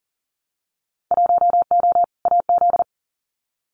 Morse code, heard as a keyed tone, about 41 wpm, 700 Hz. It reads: 1O AZ
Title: Winter Field Day Exchanges 40wpm